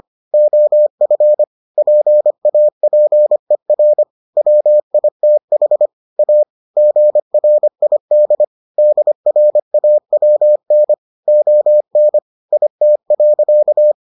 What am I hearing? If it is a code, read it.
Morse code, heard as a keyed tone, about 25 wpm, 600 Hz. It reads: OF PAPER WITH A GRID DRAWN ON IT.